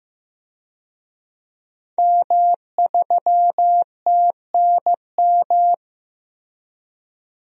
Morse code, heard as a keyed tone, about 15 wpm, 700 Hz. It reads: M3TNM
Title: DXCC-100 Call Signs - 5 Characters - Repeated 2x - 15wpm